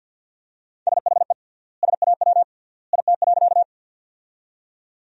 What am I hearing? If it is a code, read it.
Morse code, heard as a keyed tone, about 47 wpm, 700 Hz. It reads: SHE SAW IT.
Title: 3x Word-Spacing - Sentences from Top 300 Words - 50wpm